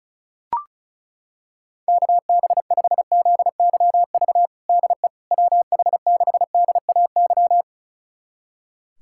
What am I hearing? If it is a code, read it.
Morse code, heard as a keyed tone, about 36 wpm, 690 Hz. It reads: KB5ZYV DE WH6DAY
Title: USA Callsign DE Callsign - 6 Characters - 35wpm